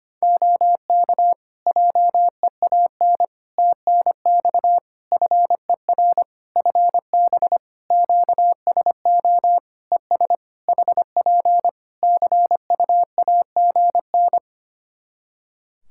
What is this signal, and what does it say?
Morse code, heard as a keyed tone, about 25 wpm, 700 Hz. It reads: OK JEAN TNX FER FB QSO ES HP CUAGN
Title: Practice QSO - Exchange 4 - Ending - 25wpm